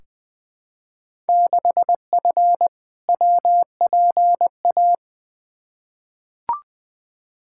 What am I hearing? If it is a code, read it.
Morse code, heard as a keyed tone, about 20 wpm, 700 Hz. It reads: 6F WPA
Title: Field Day Exchanges Encore - 20wpm